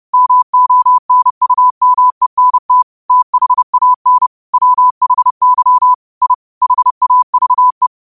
Morse code, heard as a keyed tone, about 30 wpm, 1000 Hz. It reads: MONUMENT THAN WHY I HAVE
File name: qotdfiles / 30wpm